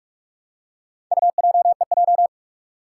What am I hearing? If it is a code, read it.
Morse code, heard as a keyed tone, about 44 wpm, 700 Hz. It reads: UJEJ
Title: Four Challenging Letter-Number Combinations 45wpm